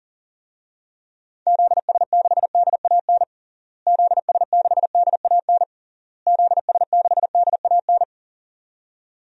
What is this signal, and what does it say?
Morse code, heard as a keyed tone, about 40 wpm, 700 Hz. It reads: ZS6DAN ZS6DAN ZS6DAN
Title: DXCC-100 Call Signs - 6 Characters - Repeated 3x - 40wpm